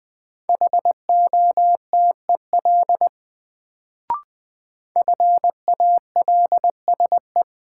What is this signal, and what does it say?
Morse code, heard as a keyed tone, about 20 wpm, 700 Hz. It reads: HOTEL FALSE
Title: Sentences from Top 1600 Words - New Word Drill - 20wpm